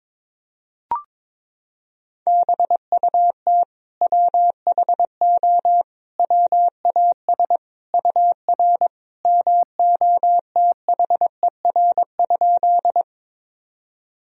Morse code, heard as a keyed tone, about 22 wpm, 700 Hz. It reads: BUT WHO WAS UR MOTHER?
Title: Sentences from Top 200 Words and Abbreviations 22wpm